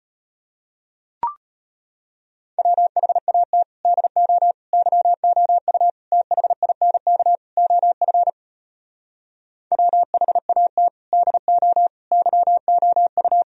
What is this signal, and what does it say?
Morse code, heard as a keyed tone, about 38 wpm, 700 Hz. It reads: WHAT DO YOU THINK OF WHAT DO YOU